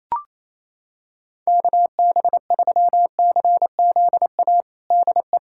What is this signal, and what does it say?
Morse code, heard as a keyed tone, about 28 wpm, 700 Hz. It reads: KB3CZA DE
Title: USA Callsign DE Callsign - 6 Characters - 28wpm